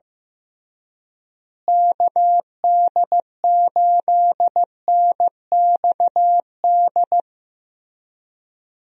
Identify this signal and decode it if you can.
Morse code, heard as a keyed tone, about 15 wpm, 700 Hz. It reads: KD8NXD
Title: USA Call Signs - 6 Characters - Repeated 3x - 15wpm